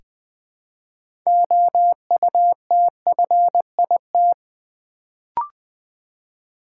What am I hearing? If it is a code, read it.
Morse code, heard as a keyed tone, about 20 wpm, 700 Hz. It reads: OUTFIT
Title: Build A Word - 2 Letters At A Time - 6-Letter Words - 20wpm